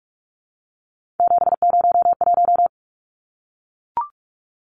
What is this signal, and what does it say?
Morse code, heard as a keyed tone, about 44 wpm, 700 Hz. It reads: 701
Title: Numbers - 3-Digit 45wpm